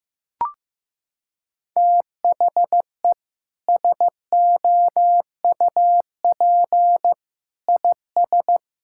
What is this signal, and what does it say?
Morse code, heard as a keyed tone, about 15 wpm, 700 Hz. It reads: THE SOUP IS
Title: Sentences from Top 1600 Words 15wpm